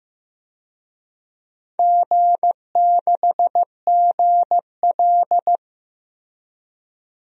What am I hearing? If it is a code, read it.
Morse code, heard as a keyed tone, about 16 wpm, 700 Hz. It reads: G6GL
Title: DXCC-100 Call Signs - 4 Characters - Repeated 2x - 15wpm